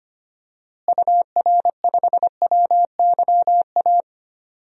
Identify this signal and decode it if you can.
Morse code, heard as a keyed tone, about 25 wpm, 700 Hz. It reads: UR5WYA
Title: DXCC-100 Call Signs - 6 Characters - Repeated 2x - 25wpm